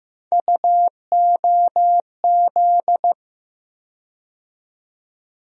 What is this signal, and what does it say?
Morse code, heard as a keyed tone, about 15 wpm, 700 Hz. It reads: UOZ
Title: USA Call Sign Postfixes - 3 Letters 15wpm